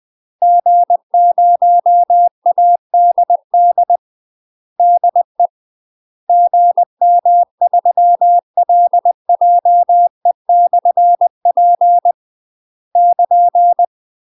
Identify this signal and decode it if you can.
Morse code, heard as a keyed tone, about 20 wpm, 700 Hz. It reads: G0ADD DE GM3LJE/P <KN>
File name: UK MASTER QSOs TRAINING / 20WPM / UK MASTER QSOs 0026